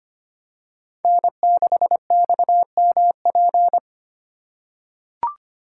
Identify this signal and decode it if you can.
Morse code, heard as a keyed tone, about 25 wpm, 700 Hz. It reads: N6XMP
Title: USA Call Signs - 3x Character Spacing - 25wpm